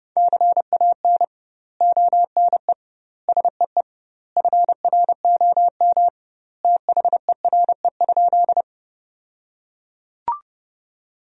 Morse code, heard as a keyed tone, about 30 wpm, 700 Hz. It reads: CAN ONE SEE FROM THERE?
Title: 2x Word-Spacing - Sentences from Top 100 Words - 30wpm